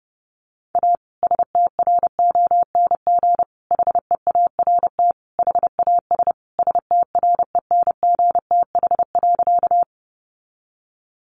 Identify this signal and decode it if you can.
Morse code, heard as a keyed tone, about 30 wpm, 700 Hz. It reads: A STRONG HEART HAS STRENGTH.
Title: Sentences from Top 1300 Words 30wpm